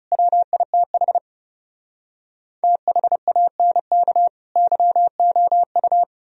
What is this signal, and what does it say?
Morse code, heard as a keyed tone, about 38 wpm, 700 Hz. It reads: WITH THANK YOU